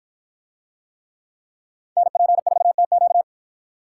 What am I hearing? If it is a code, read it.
Morse code, heard as a keyed tone, about 51 wpm, 700 Hz. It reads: NP4TQ